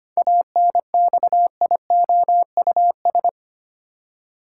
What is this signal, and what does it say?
Morse code, heard as a keyed tone, about 25 wpm, 700 Hz. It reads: ANXIOUS
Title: Seven Letter Words - 25wpm